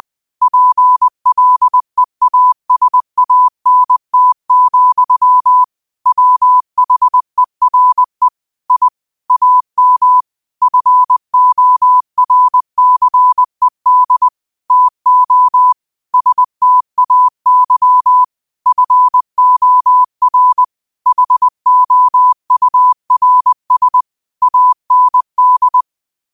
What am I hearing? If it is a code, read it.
Morse code, heard as a keyed tone, about 20 wpm, 1000 Hz. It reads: PLEASANT, WHERE I AM FORCED TO STAY FOR HOURS AND
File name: qotdfiles / 20wpm